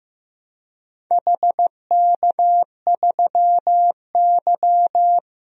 Morse code, heard as a keyed tone, about 15 wpm, 700 Hz. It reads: HK3Y
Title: DXCC-100 Call Signs - 4 Characters - 15wpm